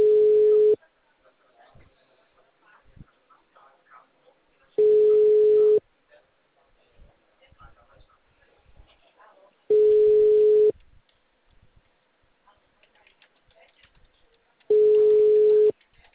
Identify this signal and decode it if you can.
Ringback tone (Europe, 425 Hz), with speech.
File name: out- -2012- - - .86460